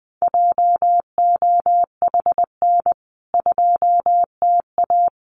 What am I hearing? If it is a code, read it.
Morse code, heard as a keyed tone, about 20 wpm, 700 Hz. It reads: JOHN 2TA
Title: CWT Exchange 20wpm